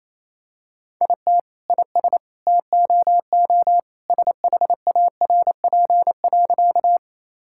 Morse code, heard as a keyed tone, about 28 wpm, 700 Hz. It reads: IT IS TOO SHARP.
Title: Sentences from Top 1600 Words 28wpm